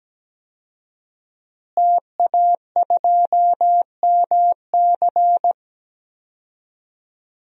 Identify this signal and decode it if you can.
Morse code, heard as a keyed tone, about 17 wpm, 700 Hz. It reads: TA2MC
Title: DXCC-100 Call Signs - 5 Characters - Repeated 3x - 17wpm